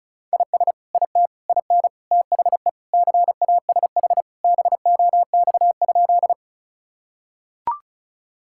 Morse code, heard as a keyed tone, about 36 wpm, 690 Hz. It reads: IS IT IN THE CASH BOX?
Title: Sentences from Top 1000 Words - Review Part 5 - 35wpm